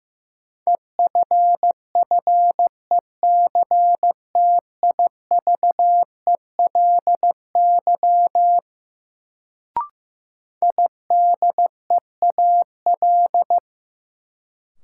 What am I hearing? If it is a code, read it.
Morse code, heard as a keyed tone, about 15 wpm, 700 Hz. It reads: EFFECTIVELY IDEAL
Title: Sentences from Top 1400 Words - New Word Drill Alt - 15wpm